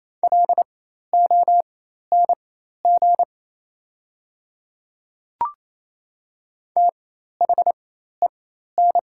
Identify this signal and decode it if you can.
Morse code, heard as a keyed tone, about 28 wpm, spaced as for 13 wpm, 700 Hz. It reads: LONG THEN
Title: 4x Character Spacing - Top 100 Words - 28wpm